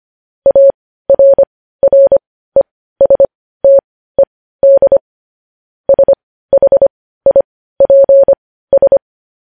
Morse code, heard as a keyed tone, about 25 wpm, spaced as for 15 wpm, 550 Hz. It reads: ARRESTED SHIPS
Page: news-headlines-in-morse-code-at-15-wpm